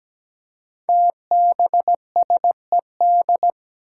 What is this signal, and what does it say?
Morse code, heard as a keyed tone, about 17 wpm, 700 Hz. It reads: TBSED
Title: Five Challenging Letter-Number Combinations 17wpm